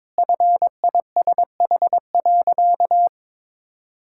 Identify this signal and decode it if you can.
Morse code, heard as a keyed tone, about 22 wpm, 690 Hz. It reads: FISH.
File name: practice / 22wpm / Easy Sentences with Hints - 22wpm